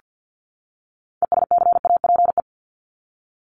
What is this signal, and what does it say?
Morse code, heard as a keyed tone, about 48 wpm, 700 Hz. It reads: ESCAPE
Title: Top 1501-1750 Words 50wpm